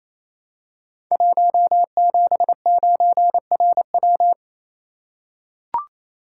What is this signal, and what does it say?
Morse code, heard as a keyed tone, about 27 wpm, 700 Hz. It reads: 179RW
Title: Code Groups - 5 Letter:Numbers - 28wpm